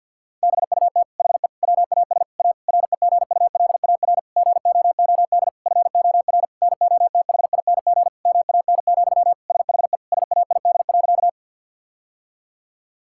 Morse code, heard as a keyed tone, about 52 wpm, 700 Hz. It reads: BUT HE WAS A REGULAR GOOD FOR NOTHING MAN, SHE SAID.
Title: Sentences from Top 1000 Words - Review Part 6 - 50wpm